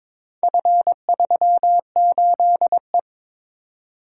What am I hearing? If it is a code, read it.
Morse code, heard as a keyed tone, about 22 wpm, 700 Hz. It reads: F38E
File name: practice / 22wpm / Four Letter-Number Combinations - 22wpm